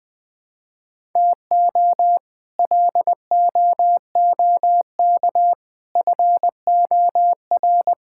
Morse code, heard as a keyed tone, about 21 wpm, 700 Hz. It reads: TO LOOK FOR
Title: Sets of 6 Words - 20wpm